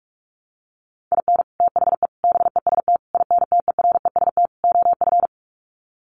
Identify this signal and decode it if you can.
Morse code, heard as a keyed tone, about 44 wpm, 700 Hz. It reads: IN THE BEST INTEREST OF
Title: Sets of 5 Words Encore 45wpm